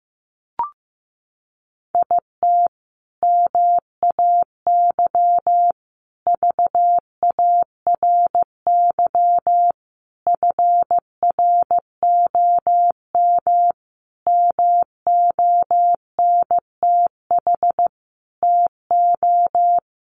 Morse code, heard as a keyed tone, about 15 wpm, 700 Hz. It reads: IT MAY VARY FROM MONTH TO
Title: Sentences from Top 1200 Words 15wpm